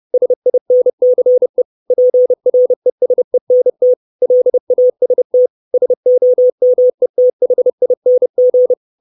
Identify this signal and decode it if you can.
Morse code, heard as a keyed tone, about 30 wpm, 500 Hz. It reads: SINCE PRESENT LAST SOMETHING
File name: 632 Words 30WPM